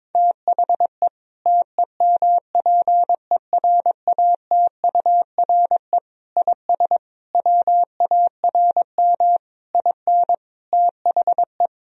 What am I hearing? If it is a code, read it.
Morse code, heard as a keyed tone, about 22 wpm, 700 Hz. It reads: THE TEMPERATURE IS WARM IN THE
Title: Sentences from Top 1000 Words - Review Part 4 - 22wpm